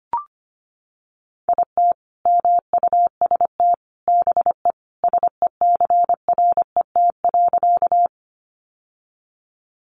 Morse code, heard as a keyed tone, about 25 wpm, 700 Hz. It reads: IT MUST BE SECRET.
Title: Sentences from Top 1200 Words 25wpm